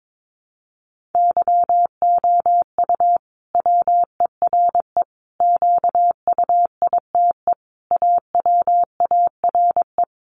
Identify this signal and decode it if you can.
Morse code, heard as a keyed tone, about 22 wpm, 700 Hz. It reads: YOU WERE QUITE AWARE
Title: Sentences from Top 1200 Words 22wpm